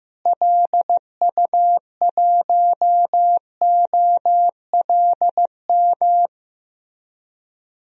Morse code, heard as a keyed tone, about 15 wpm, 700 Hz. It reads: LU1OLM
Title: DXCC-100 Call Signs - 6 Characters - Repeated 3x - 15wpm